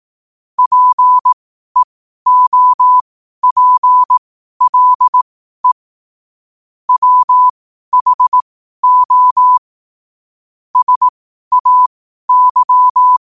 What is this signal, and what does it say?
Morse code, heard as a keyed tone, about 18 wpm, spaced as for 12 wpm, 1000 Hz. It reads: PEOPLE WHO SAY
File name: 12wpm